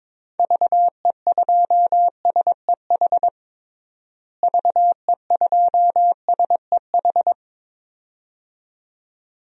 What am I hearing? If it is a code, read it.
Morse code, heard as a keyed tone, about 22 wpm, 700 Hz. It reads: VE2SEH VE2SEH
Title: DXCC-100 Call Signs - 6 Characters - Repeated 2x - 22wpm